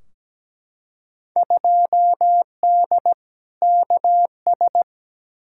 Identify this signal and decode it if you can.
Morse code, heard as a keyed tone, about 17 wpm, 700 Hz. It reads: 2D KS
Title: Field Day Exchanges Encore - 17wpm